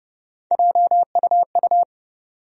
Morse code, heard as a keyed tone, about 32 wpm, 700 Hz. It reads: JUU
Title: USA Call Sign Postfixes - 3 Letters 30wpm